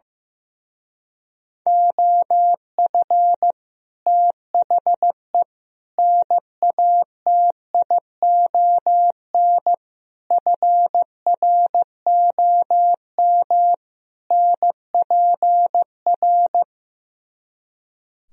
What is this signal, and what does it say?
Morse code, heard as a keyed tone, about 15 wpm, 700 Hz. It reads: OF THE NATION FROM NPR
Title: Speed-Racing - Sets of 5 Words 15wpm